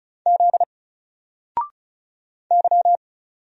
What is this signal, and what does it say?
Morse code, heard as a keyed tone, about 36 wpm, 700 Hz. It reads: Z Y
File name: Single-Letter-Number-35wpm